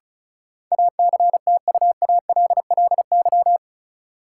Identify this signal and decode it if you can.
Morse code, heard as a keyed tone, about 36 wpm, 700 Hz. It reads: ACTUALLY
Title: Sentences from Top 700 Words - New Word Drill - 35wpm